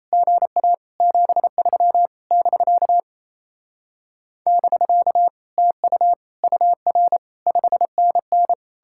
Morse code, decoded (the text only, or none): GA 73 <BK> <BK> TU UR 5NN